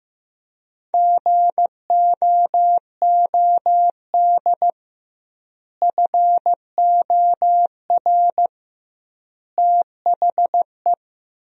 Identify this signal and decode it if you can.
Morse code, heard as a keyed tone, about 16 wpm, 700 Hz. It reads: GOOD FOR THE
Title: 2x Word-Spacing - Sentences from Top 300 Words - 15wpm